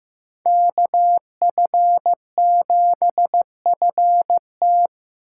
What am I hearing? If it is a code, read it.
Morse code, heard as a keyed tone, about 18 wpm, 700 Hz. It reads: KF7FT
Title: Speed Racing - USA 2x2 Call Signs 15wpm